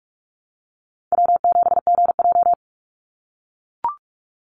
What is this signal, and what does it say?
Morse code, heard as a keyed tone, about 44 wpm, 700 Hz. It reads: W7GJ
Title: NA SOTA Chaser Callsigns 45wpm